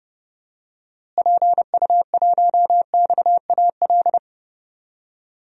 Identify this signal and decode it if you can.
Morse code, heard as a keyed tone, about 32 wpm, 700 Hz. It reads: PU1XAL
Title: DXCC-100 Call Signs - 6 Characters - Repeated 3x - 30wpm